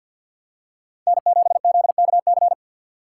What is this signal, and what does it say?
Morse code, heard as a keyed tone, about 51 wpm, 700 Hz. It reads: N7ZKC
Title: USA Call Signs - 5 Characters - 4x Character Spacing - 50wpm